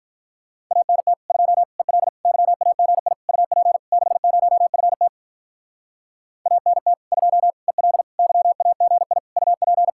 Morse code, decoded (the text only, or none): ANT 2 EL YAGI UP 60FT ANT 2 EL YAGI UP